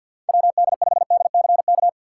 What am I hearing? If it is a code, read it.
Morse code, heard as a keyed tone, about 52 wpm, 700 Hz. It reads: WD5DCK
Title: USA Call Signs - 6 Characters - 6x Character Spacing - 50wpm